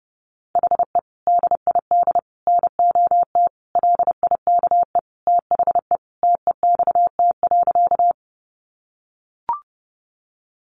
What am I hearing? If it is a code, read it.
Morse code, heard as a keyed tone, about 30 wpm, 700 Hz. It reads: HE DID NOT LIKE THE TEXT.
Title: Sentences from Top 1200 Words 30wpm